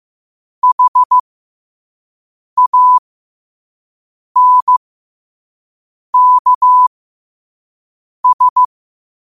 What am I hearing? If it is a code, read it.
Morse code, heard as a keyed tone, about 15 wpm, spaced as for 5 wpm, 1000 Hz. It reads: HANKS
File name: qotdfiles / 5wpm